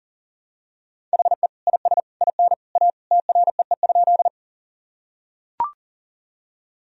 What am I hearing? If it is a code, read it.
Morse code, heard as a keyed tone, about 40 wpm, 700 Hz. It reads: HE IS IN A TREE?
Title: Sentences from Top 1000 Words - Review Part 1 - 40wpm